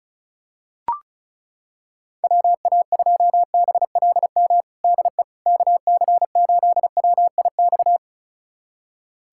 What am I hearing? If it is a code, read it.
Morse code, heard as a keyed tone, about 34 wpm, 700 Hz. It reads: WA2BLM DE KC8WIX
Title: USA Callsign DE Callsign 35wpm